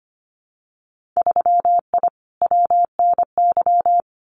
Morse code, heard as a keyed tone, about 25 wpm, 700 Hz. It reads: 3I WNY
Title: Winter Field Day Exchanges Encore 25wpm